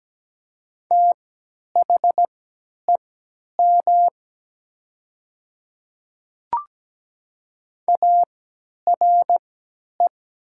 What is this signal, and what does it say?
Morse code, heard as a keyed tone, about 17 wpm, spaced as for 10 wpm, 700 Hz. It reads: THEM ARE
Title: 3x Character Spacing - Top 100 Words - 17wpm